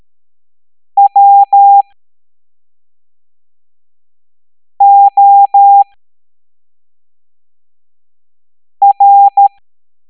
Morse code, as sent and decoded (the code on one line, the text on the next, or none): .-- --- .-.
WOR